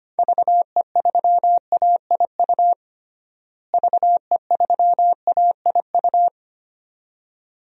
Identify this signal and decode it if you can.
Morse code, heard as a keyed tone, about 25 wpm, 700 Hz. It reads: VE3AIU VE3AIU
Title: DXCC-100 Call Signs - 6 Characters - Repeated 2x - 25wpm